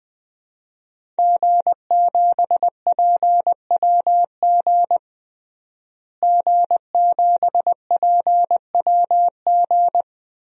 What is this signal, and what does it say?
Morse code, heard as a keyed tone, about 21 wpm, 700 Hz. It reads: G7PWG G7PWG
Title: DXCC-100 Call Signs - 5 Characters - Repeated 3x - 20wpm